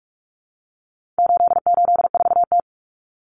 Morse code, heard as a keyed tone, about 44 wpm, 700 Hz. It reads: 884T
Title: Common Cut Numbers - 4-Digit 45wpm